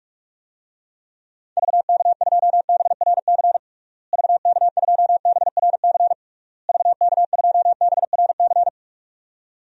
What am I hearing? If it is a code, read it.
Morse code, heard as a keyed tone, about 46 wpm, 700 Hz. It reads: VK2BRC VK2BRC VK2BRC
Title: DXCC-100 Call Signs - 6 Characters - Repeated 3x - 45wpm